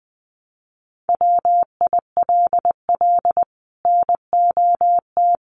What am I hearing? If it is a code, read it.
Morse code, heard as a keyed tone, about 20 wpm, 700 Hz. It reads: WILL NOT
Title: Sets of 5 Words 20wpm